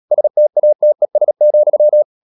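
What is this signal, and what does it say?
Morse code, heard as a keyed tone, about 37 wpm, 600 Hz. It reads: STATES,
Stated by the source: Morse code practice at 35–39 wpm sending English text of Ken Follett's novel Never, from CW med Gnister.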